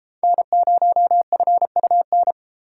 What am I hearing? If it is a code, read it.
Morse code, heard as a keyed tone, about 31 wpm, 700 Hz. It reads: N0FUN